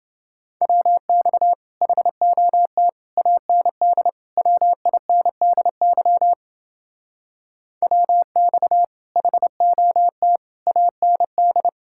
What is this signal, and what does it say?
Morse code, heard as a keyed tone, about 29 wpm, 700 Hz. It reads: WX HOT AND WINDY WX HOT AND